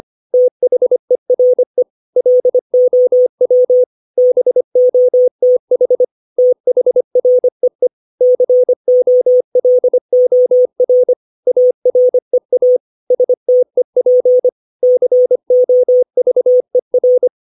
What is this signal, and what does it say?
Morse code, heard as a keyed tone, about 25 wpm, 500 Hz. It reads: THERE LOW BOTH THREE COLOR AREA STEP COVER
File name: 632 words / 632 Words 25WPM